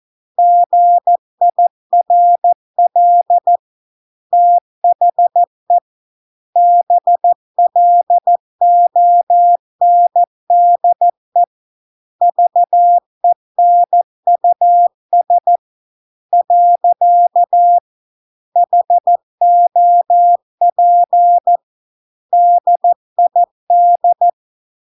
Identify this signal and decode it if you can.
Morse code, heard as a keyed tone, about 14 wpm, 700 Hz. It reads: GIRL THE BLONDE VENUS . HOP DID